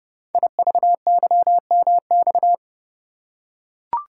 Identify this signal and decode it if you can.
Morse code, heard as a keyed tone, about 30 wpm, 700 Hz. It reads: IVYMX
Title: Code Groups - 5 Letters - 30wpm